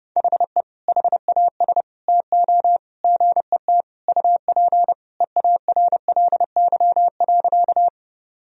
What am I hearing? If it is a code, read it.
Morse code, heard as a keyed tone, about 30 wpm, 700 Hz. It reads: HE HAS TO GET UP EARLY.
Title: Sentences from Top 1000 Words - Review Part 2 - 30wpm